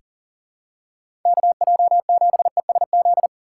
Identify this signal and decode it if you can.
Morse code, heard as a keyed tone, about 40 wpm, 700 Hz. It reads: KJ7ESZ
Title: USA Call Signs - 2x Character Spacing - 40wpm